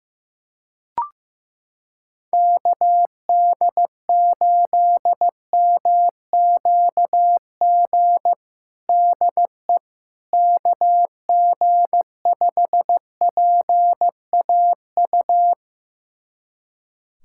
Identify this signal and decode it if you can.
Morse code, heard as a keyed tone, about 15 wpm, 700 Hz. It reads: KD8MQG DE KG5PAU
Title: USA Callsign DE Callsign 15wpm